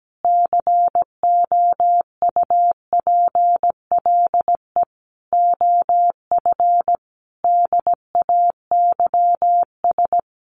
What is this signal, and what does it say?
Morse code, heard as a keyed tone, about 17 wpm, 700 Hz. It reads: COUPLE OF DAYS
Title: Sets of 5 Words Encore 17wpm